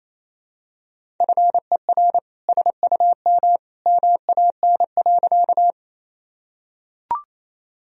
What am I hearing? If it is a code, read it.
Morse code, heard as a keyed tone, about 28 wpm, 700 Hz. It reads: FER SUM MAN.
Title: Sentences from Top 200 Words and Abbreviations 28wpm